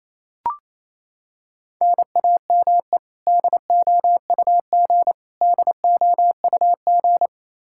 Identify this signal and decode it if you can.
Morse code, heard as a keyed tone, about 28 wpm, 700 Hz. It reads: NAME DOUG DOUG
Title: Practice QSO - Exchange 1 - Beginning - 28wpm